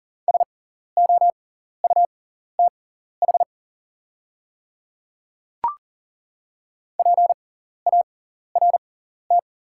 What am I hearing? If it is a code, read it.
Morse code, heard as a keyed tone, about 39 wpm, spaced as for 14 wpm, 690 Hz. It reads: SOUTH PART
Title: 6x Character Spacing - Top 500 Words - 40wpm